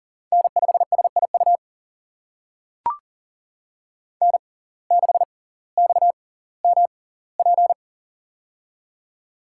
Morse code, heard as a keyed tone, about 40 wpm, spaced as for 20 wpm, 700 Hz. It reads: N5SIU N6XMP
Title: USA Call Signs - 5 Characters - 6x Character Spacing - 40wpm